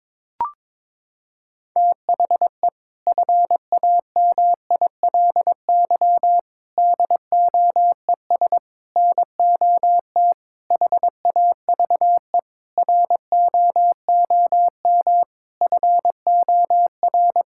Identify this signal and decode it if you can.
Morse code, heard as a keyed tone, about 22 wpm, 700 Hz. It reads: THE FAMILY DOES NOT HAVE ROOM FOR
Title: Sentences from Top 1000 Words - Review Part 5 - 22wpm